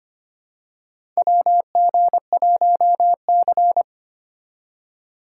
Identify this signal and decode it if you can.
Morse code, heard as a keyed tone, about 25 wpm, 700 Hz. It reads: WG1C
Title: USA Call Signs - 2x Character Spacing - 25wpm